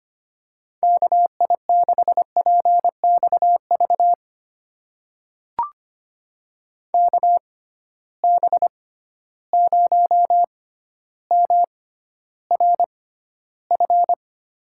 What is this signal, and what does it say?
Morse code, heard as a keyed tone, about 25 wpm, spaced as for 13 wpm, 700 Hz. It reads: KI6PXV KB0MRF
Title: USA Call Signs - 6 Characters - 6x Character Spacing - 25wpm